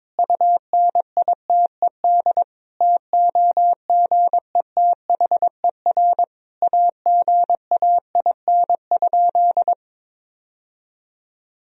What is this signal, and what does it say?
Morse code, heard as a keyed tone, about 22 wpm, 700 Hz. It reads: UNITED TOGETHER AGAIN?
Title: Sentences from Top 1000 Words - Review Part 4 - 22wpm